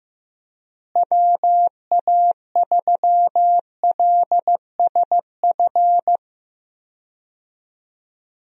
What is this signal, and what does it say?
Morse code, heard as a keyed tone, about 15 wpm, 700 Hz. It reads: WA3LSF
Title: USA Call Signs - 6 Characters - Repeated 2x - 15wpm